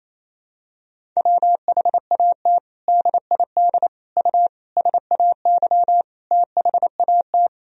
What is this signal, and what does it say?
Morse code, heard as a keyed tone, about 28 wpm, 700 Hz. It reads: WHAT DID U SAY THAT
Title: Sentences from Top 200 Words and Abbreviations 28wpm